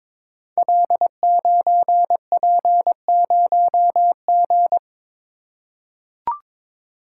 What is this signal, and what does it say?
Morse code, heard as a keyed tone, about 22 wpm, 700 Hz. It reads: L9P0G
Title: Code Groups - 5 Letter:Numbers - 22wpm